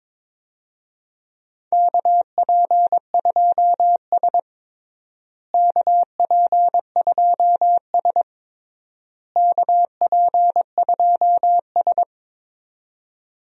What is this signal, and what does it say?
Morse code, heard as a keyed tone, about 22 wpm, 700 Hz. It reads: KP2S KP2S KP2S
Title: DXCC-100 Call Signs - 4 Characters - Repeated 3x - 22wpm